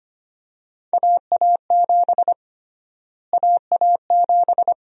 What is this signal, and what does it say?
Morse code, heard as a keyed tone, about 25 wpm, 700 Hz. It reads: AA7 AA7
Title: USA Call Sign Prefixes - Repeated 2x - 25wpm